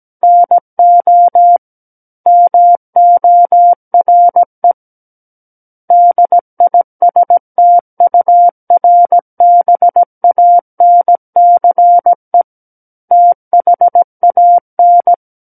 Morse code, heard as a keyed tone, about 17 wpm, 700 Hz. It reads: NO MORE DISTURBANCE THAN
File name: Book One / 08-Chapter 8 - 17 WPM